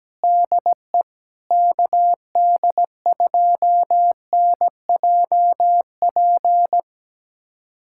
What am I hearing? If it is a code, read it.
Morse code, heard as a keyed tone, about 17 wpm, 700 Hz. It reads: DE KD2NJP
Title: USA Callsign DE Callsign - 6 Characters - 17wpm